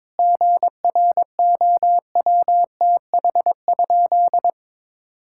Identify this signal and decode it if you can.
Morse code, heard as a keyed tone, about 22 wpm, 700 Hz. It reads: GROWTH?
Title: Sentences from Top 1000 Words - Review Part 5 - 22wpm